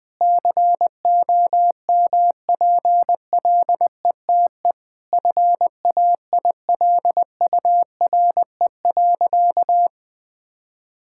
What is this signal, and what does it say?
Morse code, heard as a keyed tone, about 20 wpm, 700 Hz. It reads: COMPLETE FAILURE.
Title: Sentences from Top 1000 Words - Review Part 6 - 20wpm